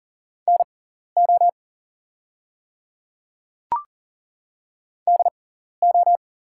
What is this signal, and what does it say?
Morse code, heard as a keyed tone, about 40 wpm, spaced as for 14 wpm, 690 Hz. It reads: NO DO
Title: 6x Character Spacing - Top 300 Words - 40wpm